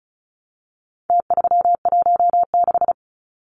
Morse code, heard as a keyed tone, about 34 wpm, 700 Hz. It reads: T316
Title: Numbers - Contest Style - 4-Digit 35wpm